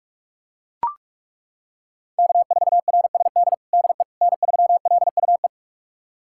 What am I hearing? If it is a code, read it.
Morse code, heard as a keyed tone, about 41 wpm, 700 Hz. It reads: K4RSD DE N3LUE
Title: USA Callsign DE Callsign - 5 Characters - 45wpm